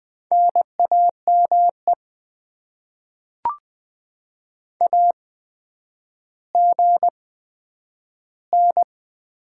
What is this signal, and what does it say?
Morse code, heard as a keyed tone, about 20 wpm, 700 Hz. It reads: N A M E A G N
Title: Top 20 QSO Elements - 8x Character Spacing - 20wpm